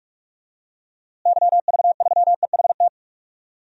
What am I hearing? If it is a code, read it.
Morse code, heard as a keyed tone, about 46 wpm, 700 Hz. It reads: YV3EHT
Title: DXCC-100 Call Signs - 6 Characters - Repeated 3x - 45wpm